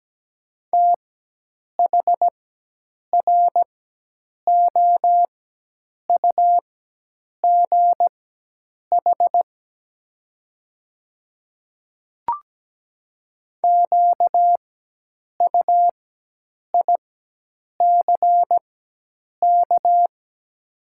Morse code, heard as a keyed tone, about 17 wpm, spaced as for 8 wpm, 700 Hz. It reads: THROUGH QUICK
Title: 4x Character Spacing - Top 500 Words - 17wpm